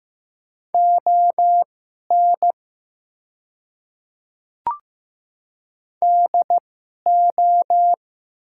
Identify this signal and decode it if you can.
Morse code, heard as a keyed tone, about 15 wpm, spaced as for 11 wpm, 700 Hz. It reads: ON DO
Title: 2x Character Spacing - Top 200 Words - 15wpm